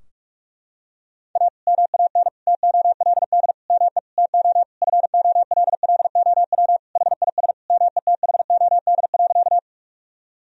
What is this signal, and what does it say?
Morse code, heard as a keyed tone, about 44 wpm, 700 Hz. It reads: A MAN TOLD ME TO FOLLOW HIS METHOD.
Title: Sentences from Top 1000 Words - Review Part 3 - 45wpm